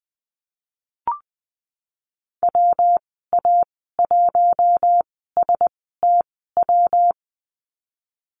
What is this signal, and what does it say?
Morse code, heard as a keyed tone, about 20 wpm, spaced as for 15 wpm, 700 Hz. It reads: WA1STW
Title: 193 Lesson 31g - Letter v - Call Signs 20wpm